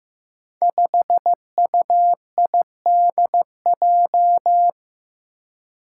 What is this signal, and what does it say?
Morse code, heard as a keyed tone, about 15 wpm, 700 Hz. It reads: 5UIDJ
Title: Five Challenging Letter-Number Combinations 15wpm